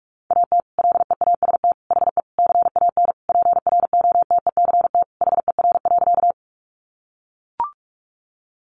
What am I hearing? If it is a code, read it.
Morse code, heard as a keyed tone, about 44 wpm, 700 Hz. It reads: AT LEAST HE CAN PROTECT HER.
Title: Sentences from Top 900 Words 45wpm